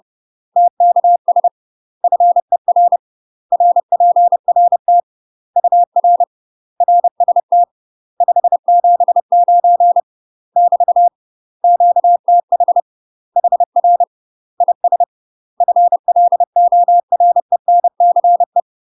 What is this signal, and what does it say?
Morse code, heard as a keyed tone, about 30 wpm, 700 Hz. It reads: TKS FER RPRT UR RST 579 <BT> QTH HR IS FLORENCE
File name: UK MASTER QSOs TRAINING / 30WPM / UK MASTER QSOs 0047